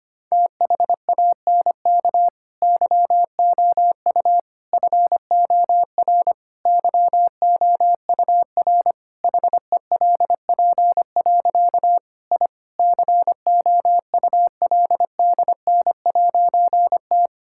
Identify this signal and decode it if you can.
Morse code, heard as a keyed tone, about 25 wpm, 700 Hz. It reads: THANK YOU FOR YOUR HELP. I COULDN'T
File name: practice / 25wpm / Difficult Sentences with Hints - 25wpm